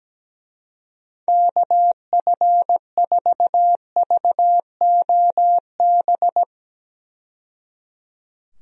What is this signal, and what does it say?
Morse code, heard as a keyed tone, about 17 wpm, 700 Hz. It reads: KF4VOB
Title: USA Call Signs - 6 Characters - 17wpm